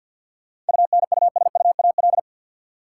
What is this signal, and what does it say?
Morse code, heard as a keyed tone, about 53 wpm, 700 Hz. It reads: UNUSUAL
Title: Sentences from Top 1400 Words - New Word Drill Alt - 50wpm